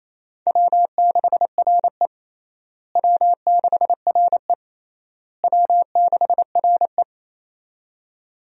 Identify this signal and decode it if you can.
Morse code, heard as a keyed tone, about 28 wpm, 700 Hz. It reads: W6RE W6RE W6RE
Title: USA Call Signs - 4 Characters - Repeated 3x - 28wpm